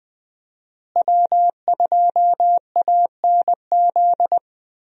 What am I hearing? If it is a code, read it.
Morse code, heard as a keyed tone, about 20 wpm, 700 Hz. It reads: W2ANZ
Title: USA Call Signs - 5 Characters - Repeated 3x - 20wpm